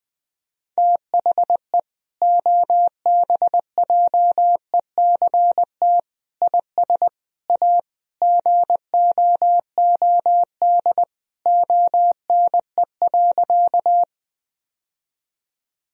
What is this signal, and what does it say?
Morse code, heard as a keyed tone, about 20 wpm, 700 Hz. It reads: THE OBJECT IS A GOOD ONE.
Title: Sentences from Top 1000 Words - Review Part 3 - 20wpm